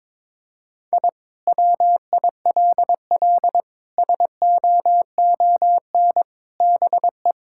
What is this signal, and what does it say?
Morse code, heard as a keyed tone, about 22 wpm, 700 Hz. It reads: I WILL SOON BE
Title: Sentences from Top 1000 Words - Review Part 6 - 22wpm